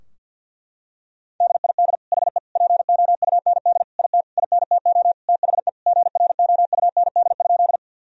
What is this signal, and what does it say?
Morse code, heard as a keyed tone, about 50 wpm, 700 Hz. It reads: DID HE POUND IT INTO THE GROUND?
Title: Sentences from Top 1000 Words - Review Part 2 - 50wpm